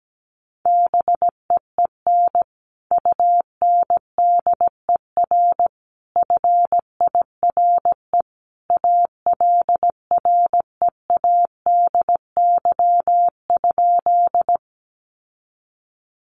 Morse code, heard as a keyed tone, about 17 wpm, 700 Hz. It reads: BEEN UNDER FIRE ALREADY?
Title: Sentences from Top 700 Words 17wpm